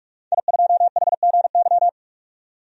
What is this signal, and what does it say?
Morse code, heard as a keyed tone, about 41 wpm, 700 Hz. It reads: I2HGY
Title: DXCC-100 Call Signs - 5 Characters - 45wpm